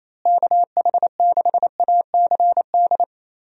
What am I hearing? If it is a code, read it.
Morse code, heard as a keyed tone, about 28 wpm, 700 Hz. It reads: KH6ACD
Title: USA Call Signs - 6 Characters - Repeated 3x - 28wpm